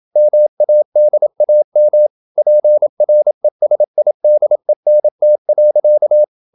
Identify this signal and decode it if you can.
Morse code, heard as a keyed tone, about 28 wpm, 600 Hz. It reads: MADAM PRESIDENT.
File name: Never 0027wpm